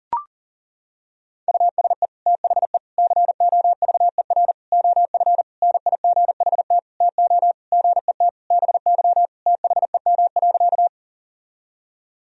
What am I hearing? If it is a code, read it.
Morse code, heard as a keyed tone, about 40 wpm, 700 Hz. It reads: USE THE COVER OF NIGHT TO GET BY THEM.
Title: Sentences from Top 1000 Words - Review Part 1 - 40wpm